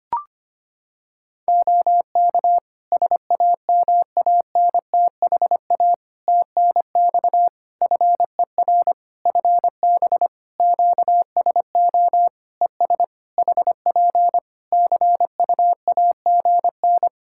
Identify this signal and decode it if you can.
Morse code, heard as a keyed tone, about 25 wpm, 700 Hz. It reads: OK SAMANTHA TNX FER FB QSO ES HP CUAGN
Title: Practice QSO - Exchange 4 - Ending - 25wpm